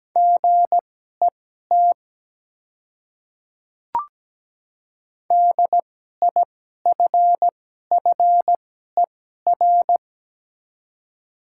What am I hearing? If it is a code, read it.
Morse code, heard as a keyed tone, about 17 wpm, spaced as for 12 wpm, 700 Hz. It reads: GET DIFFER
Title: 2x Character Spacing - Top 200 Words - 17wpm